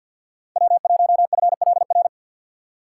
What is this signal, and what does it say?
Morse code, heard as a keyed tone, about 53 wpm, 700 Hz. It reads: W1FLR
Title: USA Call Signs - 5 Characters - 2x Character Spacing - 50wpm